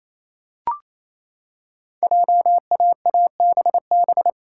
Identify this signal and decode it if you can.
Morse code, heard as a keyed tone, about 27 wpm, 700 Hz. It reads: JAABB
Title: Code Groups - 5 Letters - 28wpm